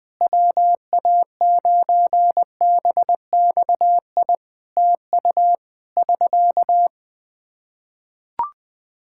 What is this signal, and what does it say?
Morse code, heard as a keyed tone, about 20 wpm, 700 Hz. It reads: WA9BXI TU <SK>
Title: Practice QSO - Exchange 4 - Ending - 20wpm